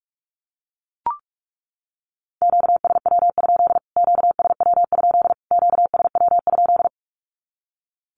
Morse code, heard as a keyed tone, about 44 wpm, 700 Hz. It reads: QSW? QSW? QSW?
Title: Q-Codes Repeated 3x 45wpm